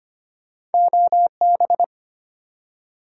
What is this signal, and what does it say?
Morse code, heard as a keyed tone, about 36 wpm, spaced as for 31 wpm, 700 Hz. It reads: OB
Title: Speed-Racing Abbreviations - 25wpm